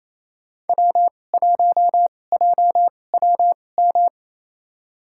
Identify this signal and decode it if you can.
Morse code, heard as a keyed tone, about 28 wpm, spaced as for 20 wpm, 700 Hz. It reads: W1JWM
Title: USA Call Signs - 5 Characters - 2x Character Spacing - 28wpm